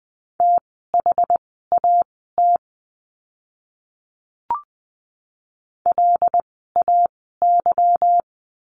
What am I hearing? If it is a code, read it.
Morse code, heard as a keyed tone, about 20 wpm, spaced as for 15 wpm, 700 Hz. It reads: THAT LAY
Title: 146 Lesson 24f - Letter y - Words 20wpm